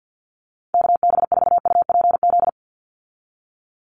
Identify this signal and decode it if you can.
Morse code, heard as a keyed tone, about 48 wpm, 700 Hz. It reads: KB4UPZ
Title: USA Call Signs 50wpm